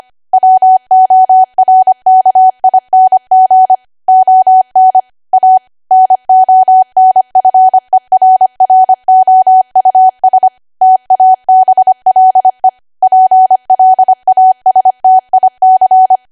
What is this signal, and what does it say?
Morse code, heard as a keyed tone, about 25 wpm, 750 Hz. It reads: WORKING ON A NONFERROUS TABLE PLASTIC